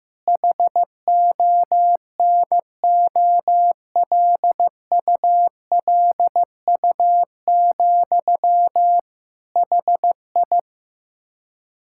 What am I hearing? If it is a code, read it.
Morse code, heard as a keyed tone, about 15 wpm, 700 Hz. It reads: HONOLULU, HI
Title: Practice QSO - Exchange 1 - Beginning - 15wpm